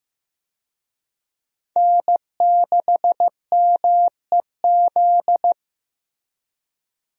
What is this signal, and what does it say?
Morse code, heard as a keyed tone, about 15 wpm, 700 Hz. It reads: N6MEZ
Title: USA Call Signs - 5 Characters - Repeated 2x - 15wpm